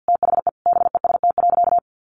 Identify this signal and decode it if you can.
Morse code, heard as a keyed tone, about 50 wpm, 700 Hz. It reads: THE BEST.
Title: Sentences from Top 600 Words 50wpm